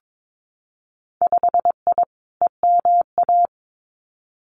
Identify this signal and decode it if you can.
Morse code, heard as a keyed tone, about 22 wpm, 700 Hz. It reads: 5I EMA
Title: Winter Field Day Exchanges Encore 22wpm